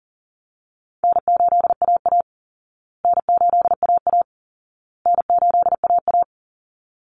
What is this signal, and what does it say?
Morse code, heard as a keyed tone, about 40 wpm, 700 Hz. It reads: N8AA N8AA N8AA
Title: Easy Call Signs Repeated 3x - 40wpm